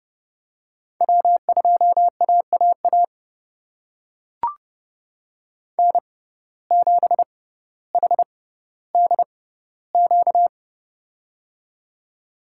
Morse code, heard as a keyed tone, about 29 wpm, spaced as for 15 wpm, 700 Hz. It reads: W2AAA N7HDQ
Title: USA Call Signs - 5 Characters - 6x Character Spacing - 30wpm